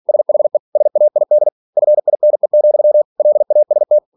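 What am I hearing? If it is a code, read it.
Morse code, heard as a keyed tone, about 46 wpm, 600 Hz. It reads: SHE SAID FINE, LAST
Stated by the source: Practice text from an English novel sent at 47 WPM.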